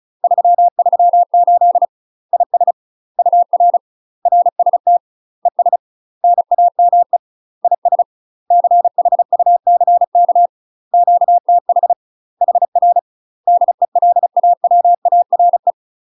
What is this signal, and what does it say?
Morse code, heard as a keyed tone, about 36 wpm, 700 Hz. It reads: 338 IS UR RST ES NAME IS CHUCK QTH HR DELAWARE
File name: UK MASTER QSOs TRAINING / 35WPM / UK MASTER QSOs 0007